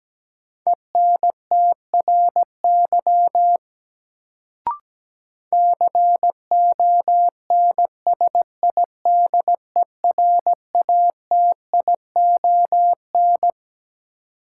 Morse code, heard as a keyed tone, about 17 wpm, 700 Hz. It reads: ENTRY CONSIDERATION
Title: Sentences from Top 1500 Words - New Word Drill Alt - 17wpm